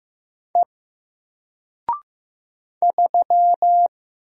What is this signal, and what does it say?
Morse code, heard as a keyed tone, about 15 wpm, 700 Hz. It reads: E 3
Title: Single Character (Letter:Number:Punctuation:Prosign) Alt 15wpm